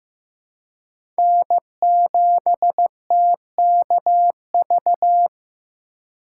Morse code, heard as a keyed tone, about 15 wpm, 700 Hz. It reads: N7TKV
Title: USA Call Signs - 5 Characters - 2x Character Spacing - 15wpm